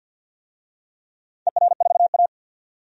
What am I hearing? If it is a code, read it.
Morse code, heard as a keyed tone, about 53 wpm, 700 Hz. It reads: ER4A
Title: DXCC-100 Call Signs - 4 Characters - 50wpm